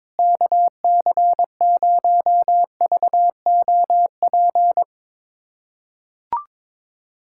Morse code, heard as a keyed tone, about 21 wpm, 700 Hz. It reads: KC0VOP
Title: USA Callsign DE Callsign 22wpm